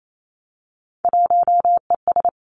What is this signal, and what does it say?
Morse code, heard as a keyed tone, about 28 wpm, 700 Hz. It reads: 1ES
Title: Three Challenging Letter-Number Combinations 28wpm